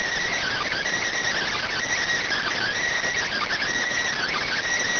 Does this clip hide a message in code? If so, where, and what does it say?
Yes: Emergency Alert System (SAME) tones (header bursts) from 0.3 to 4.6 s, heard as harsh buzzing bursts.